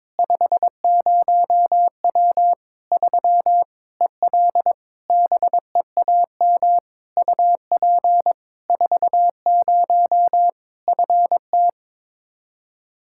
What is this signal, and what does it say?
Morse code, heard as a keyed tone, about 21 wpm, 700 Hz. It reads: 50W 3 EL BEAM UP 40 FT
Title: Practice QSO - Exchange 2 - Info - 22wpm